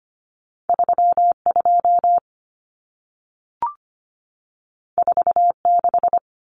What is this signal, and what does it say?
Morse code, heard as a keyed tone, about 25 wpm, 700 Hz. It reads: 32 46
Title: Numbers - 2-Digit 25wpm